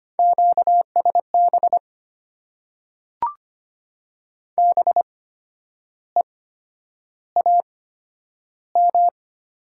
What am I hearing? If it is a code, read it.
Morse code, heard as a keyed tone, about 25 wpm, spaced as for 10 wpm, 690 Hz. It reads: QSB BEAM
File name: Top 90 QSO Elements - 8x Character Spacing - 25wpm